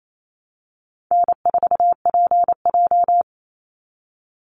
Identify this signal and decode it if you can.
Morse code, heard as a keyed tone, about 28 wpm, 700 Hz. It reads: N4PJ
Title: Easy Call Signs Repeated 3x - 28wpm